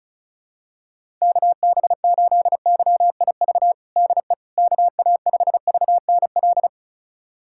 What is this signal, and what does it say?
Morse code, heard as a keyed tone, about 36 wpm, 700 Hz. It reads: KB8YIV DE KA5VNL
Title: USA Callsign DE Callsign - 6 Characters - 35wpm